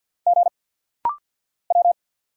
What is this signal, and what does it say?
Morse code, heard as a keyed tone, about 46 wpm, 690 Hz. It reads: G W
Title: Single Letters - Rapid-Fire 50wpm